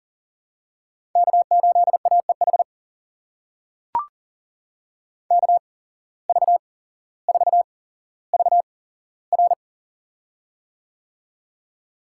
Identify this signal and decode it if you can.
Morse code, heard as a keyed tone, about 38 wpm, spaced as for 17 wpm, 700 Hz. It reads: K8AEH KV4VR
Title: USA Call Signs - 8x Character Spacing - 40wpm